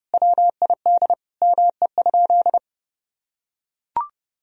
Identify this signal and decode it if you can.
Morse code, heard as a keyed tone, about 30 wpm, 700 Hz. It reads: WID ME?
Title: Sentences from Top 200 Words and Abbreviations 30wpm